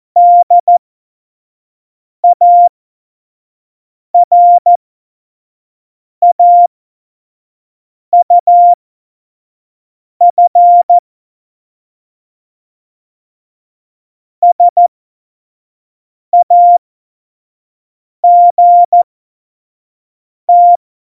Morse code, heard as a keyed tone, about 14 wpm, spaced as for 5 wpm, 700 Hz. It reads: DARAUF SAGT
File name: Morsekurs / Witze / Witz 26. Februar 2019, 25 F70